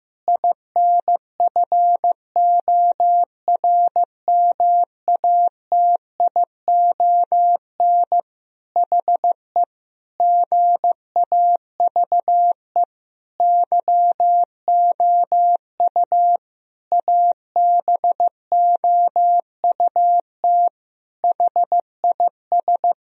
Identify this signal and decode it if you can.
Morse code, heard as a keyed tone, about 15 wpm, 700 Hz. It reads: INFORMATION HE GAVE YOU ABOUT HIS
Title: Sentences from Top 1000 Words - Review Part 3 - 15wpm